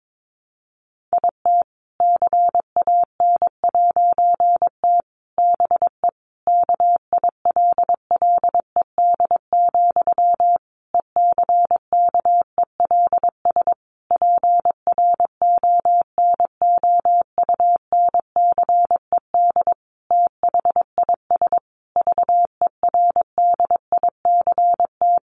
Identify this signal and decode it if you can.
Morse code, heard as a keyed tone, about 22 wpm, 700 Hz. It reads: IT CAN'T BE KILLED, ECKELS PRONOUNCED THIS VERDICT
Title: A Sound Of Thunder 22wpm